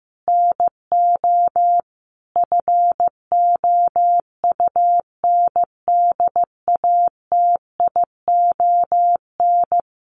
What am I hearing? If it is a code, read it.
Morse code, heard as a keyed tone, about 15 wpm, 700 Hz. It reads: NO FOUNDATION
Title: Sentences from Top 1200 Words 15wpm